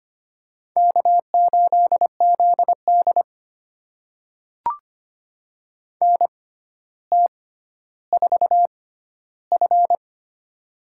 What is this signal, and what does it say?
Morse code, heard as a keyed tone, about 25 wpm, spaced as for 13 wpm, 700 Hz. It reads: K8ZD NT4F
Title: USA Call Signs - 4 Characters - 6x Character Spacing - 25wpm